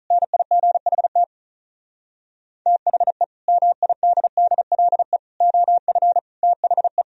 Morse code, heard as a keyed tone, about 44 wpm, 700 Hz. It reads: NIGHT THE MIDDLE OF THE